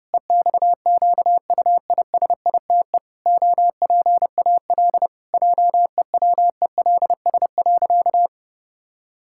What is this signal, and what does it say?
Morse code, heard as a keyed tone, about 32 wpm, 700 Hz. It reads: EXQUISITE OPAL JEWELS.
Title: Pangram Challenge v2 - 30wpm